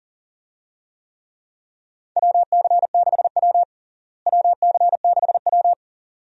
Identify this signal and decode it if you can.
Morse code, heard as a keyed tone, about 37 wpm, 700 Hz. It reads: WC6W WC6W
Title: USA Call Signs - 4 Characters - Repeated 2x - 40wpm